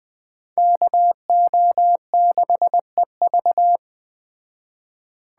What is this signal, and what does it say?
Morse code, heard as a keyed tone, about 20 wpm, 700 Hz. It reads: KO6EV
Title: Speed Racing - USA 2x2 Call Signs 17wpm